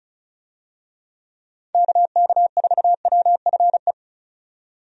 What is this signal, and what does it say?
Morse code, heard as a keyed tone, about 36 wpm, 700 Hz. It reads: KK4WFE
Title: USA Call Signs - 6 Characters - 35wpm